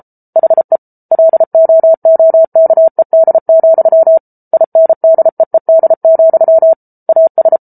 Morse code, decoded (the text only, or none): HE LOOKED, INDEED, AS